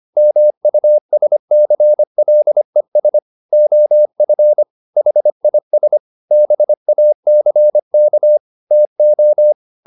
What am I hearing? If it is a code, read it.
Morse code, heard as a keyed tone, about 25 wpm, 600 Hz. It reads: MUSCLES OF HIS BACK TO
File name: the old man and the sea part 1 25wpm